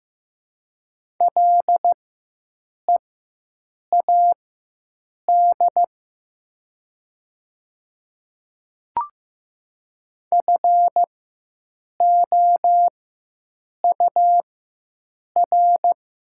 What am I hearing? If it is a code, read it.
Morse code, heard as a keyed tone, about 15 wpm, spaced as for 7 wpm, 690 Hz. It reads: LEAD FOUR
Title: 4x Character Spacing - Top 500 Words - 15wpm